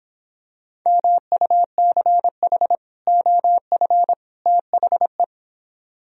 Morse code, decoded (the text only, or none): MUCH OF THE